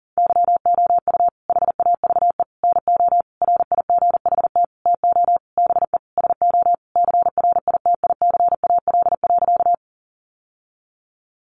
Morse code, heard as a keyed tone, about 40 wpm, 700 Hz. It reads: YOU HAVE NO RIGHT TO BE SO CRITICAL.
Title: Sentences from Top 1100 Words 40wpm